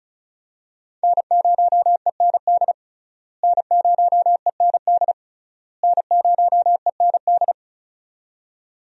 Morse code, heard as a keyed tone, about 34 wpm, 700 Hz. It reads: N0END N0END N0END
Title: USA Call Signs - 5 Characters - Repeated 3x - 35wpm